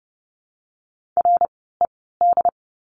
Morse code, heard as a keyed tone, about 30 wpm, spaced as for 17 wpm, 700 Hz. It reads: RED
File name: Lesson 16f - Letter c - Words 30wpm